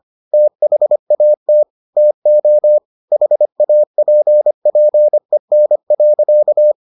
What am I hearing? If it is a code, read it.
Morse code, heard as a keyed tone, about 25 wpm, 600 Hz. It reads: THAT TO HAPPEN.